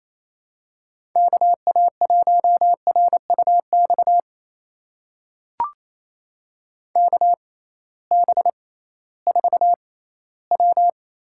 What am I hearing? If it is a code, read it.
Morse code, heard as a keyed tone, about 28 wpm, spaced as for 16 wpm, 700 Hz. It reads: KA1RUX KB4W
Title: USA Call Signs - 6 Characters - 6x Character Spacing - 28wpm